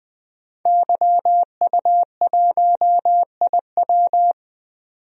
Morse code, heard as a keyed tone, about 20 wpm, 700 Hz. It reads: YU1IW
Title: DXCC-100 Call Signs - 5 Characters - 20wpm